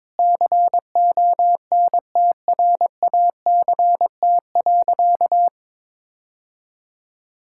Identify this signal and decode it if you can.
Morse code, heard as a keyed tone, about 21 wpm, 700 Hz. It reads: CONTRACT.
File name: practice / 22wpm / Sentences with 8-Word Phrases - 22wpm